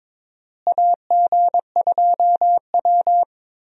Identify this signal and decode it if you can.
Morse code, heard as a keyed tone, about 22 wpm, 700 Hz. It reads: AG2W
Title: USA Call Signs - 4 Characters - Repeated 3x - 22wpm